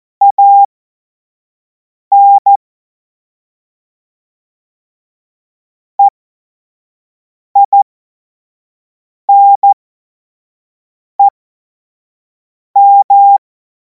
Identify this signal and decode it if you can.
Morse code, heard as a keyed tone, about 14 wpm, spaced as for 5 wpm, 800 Hz. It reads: AN EINEM
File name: Morsekurs / Witze / Witz 03. Februar 2026 25 F70